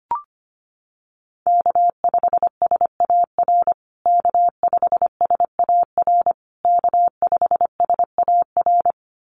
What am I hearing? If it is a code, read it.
Morse code, heard as a keyed tone, about 25 wpm, 700 Hz. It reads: K5SAR K5SAR K5SAR
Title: USA Call Signs Repeated 3x 25wpm